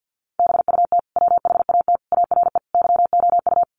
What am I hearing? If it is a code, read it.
Morse code, heard as a keyed tone, about 50 wpm, 700 Hz. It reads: BUT WHAT ARE YOU
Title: Sentences from Top 100 Words 50wpm